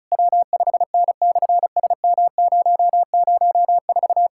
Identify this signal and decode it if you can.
Morse code, heard as a keyed tone, about 33 wpm, 700 Hz. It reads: W5N/SM004
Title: SOTA QSO Practice 35wpm